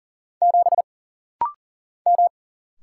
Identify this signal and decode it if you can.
Morse code, heard as a keyed tone, about 40 wpm, 700 Hz. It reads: 7 M
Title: Single Letter-Number - Rapid-Fire 40wpm